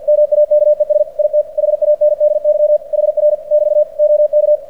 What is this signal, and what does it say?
Morse code, heard as a keyed tone, about 46 wpm, 600 Hz. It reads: MAKES IT HANDY HR <BT> OK
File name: archived / CRSnet